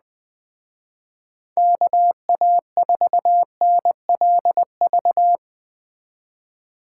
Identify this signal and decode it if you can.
Morse code, heard as a keyed tone, about 20 wpm, 700 Hz. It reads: KA4NLV